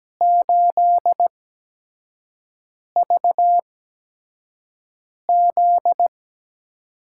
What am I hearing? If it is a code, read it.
Morse code, heard as a keyed tone, about 17 wpm, spaced as for 5 wpm, 700 Hz. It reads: 8VZ
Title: USA Call Signs - 8x Character Spacing - 17wpm